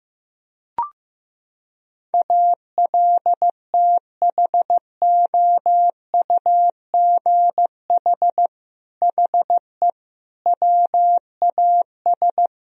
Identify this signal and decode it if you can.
Morse code, heard as a keyed tone, about 15 wpm, 700 Hz. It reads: ALTHOUGH HE WAS
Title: Sentences from Top 1000 Words - Review Part 3 - 15wpm